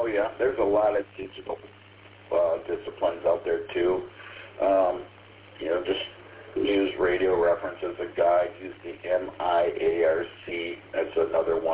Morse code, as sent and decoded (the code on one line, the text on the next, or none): none